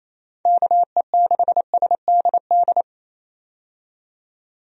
Morse code, heard as a keyed tone, about 28 wpm, 700 Hz. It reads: KE6SDD
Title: USA Call Signs - 6 Characters - 28wpm